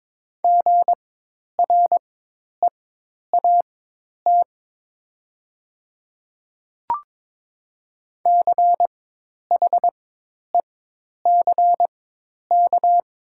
Morse code, heard as a keyed tone, about 22 wpm, spaced as for 10 wpm, 700 Hz. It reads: GREAT CHECK
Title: 4x Character Spacing - Top 500 Words - 22wpm